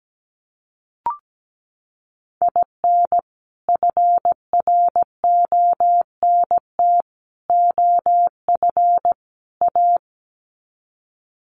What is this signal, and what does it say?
Morse code, heard as a keyed tone, about 17 wpm, 700 Hz. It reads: IN FRONT OF A
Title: Sets of 4 Words 17wpm